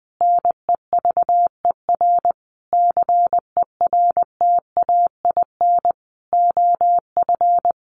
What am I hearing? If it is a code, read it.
Morse code, heard as a keyed tone, about 20 wpm, 700 Hz. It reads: NEVER CERTAIN OF
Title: Sentences from Top 1300 Words 20wpm